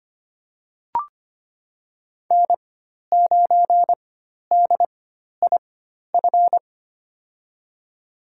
Morse code, heard as a keyed tone, about 25 wpm, spaced as for 12 wpm, 700 Hz. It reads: N9DIF
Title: USA Call Signs - 5 Characters - 4x Character Spacing - 25wpm